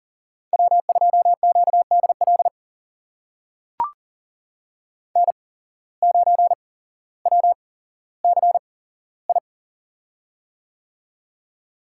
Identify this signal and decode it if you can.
Morse code, heard as a keyed tone, about 39 wpm, spaced as for 17 wpm, 700 Hz. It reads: W2QDL N9WCI
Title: USA Call Signs - 5 Characters - 8x Character Spacing - 40wpm